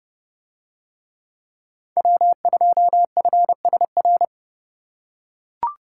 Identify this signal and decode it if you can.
Morse code, heard as a keyed tone, about 31 wpm, 700 Hz. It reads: W2FSR